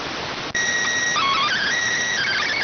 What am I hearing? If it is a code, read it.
SSTV (slow-scan television), PD 290.